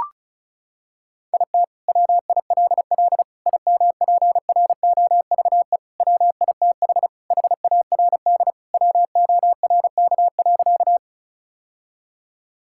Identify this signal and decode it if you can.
Morse code, heard as a keyed tone, about 36 wpm, 700 Hz. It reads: IT WILL IMPROVE WITH HARD WORK.
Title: Sentences from Top 1000 Words - Review Part 5 - 35wpm